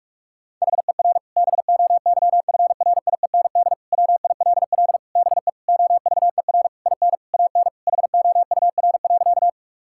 Morse code, heard as a keyed tone, about 46 wpm, 690 Hz. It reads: HER BOYFRIEND WILL BE OVER IN AN HOUR.
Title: Sentences from Top 1000 Words - Review Part 5 - 45wpm